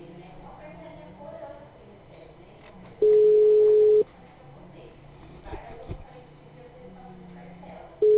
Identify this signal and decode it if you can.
Ringback tone (Europe, 425 Hz), with speech.
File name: out- -2014- - - .6070